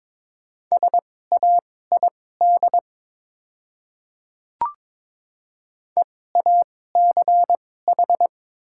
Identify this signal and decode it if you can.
Morse code, heard as a keyed tone, about 22 wpm, spaced as for 16 wpm, 700 Hz. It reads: SAID EACH
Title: 2x Character Spacing - Top 200 Words - 22wpm